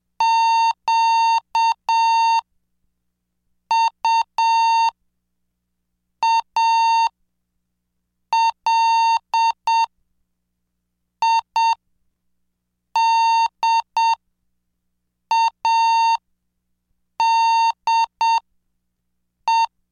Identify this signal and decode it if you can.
Morse code, heard as a keyed tone, about 7 wpm, spaced as for 5 wpm, 910 Hz. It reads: QUALIDADE
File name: cursos / CDestudo / 20 Track 20